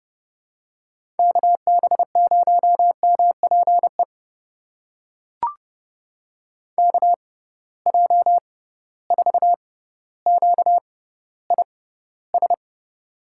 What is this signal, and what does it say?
Morse code, heard as a keyed tone, about 31 wpm, spaced as for 15 wpm, 700 Hz. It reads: KB0MPE KJ4QIS
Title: USA Call Signs - 6 Characters - 6x Character Spacing - 30wpm